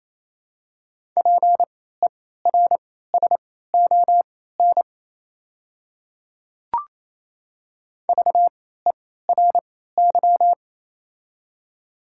Morse code, heard as a keyed tone, about 28 wpm, spaced as for 16 wpm, 700 Hz. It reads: PERSON VERY
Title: 3x Character Spacing - Top 500 Words - 28wpm